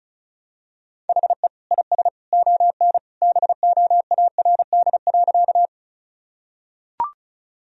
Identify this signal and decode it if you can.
Morse code, heard as a keyed tone, about 36 wpm, 700 Hz. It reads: HE IS ON BOARD.
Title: Sentences from Top 1000 Words - Review Part 4 - 35wpm